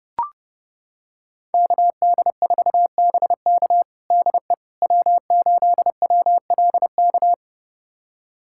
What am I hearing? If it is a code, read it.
Morse code, heard as a keyed tone, about 30 wpm, 700 Hz. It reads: KD4BK DE W8WLK
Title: USA Callsign DE Callsign 30wpm